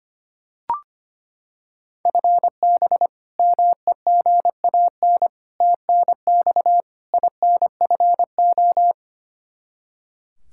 Morse code, heard as a keyed tone, about 25 wpm, 700 Hz. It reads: FB MEGAN TNX INFO
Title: Practice QSO - Exchange 3 - More Info - 25wpm